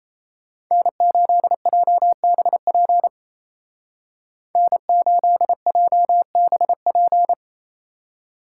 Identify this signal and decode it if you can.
Morse code, heard as a keyed tone, about 35 wpm, 700 Hz. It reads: N8JBP N8JBP
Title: Speed Racing - USA 1x3 Call Signs 28wpm